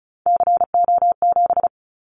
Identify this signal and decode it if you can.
Morse code, heard as a keyed tone, about 36 wpm, 700 Hz. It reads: CO7
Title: Callsign Prefixes for Top 100 DXCC - 35wpm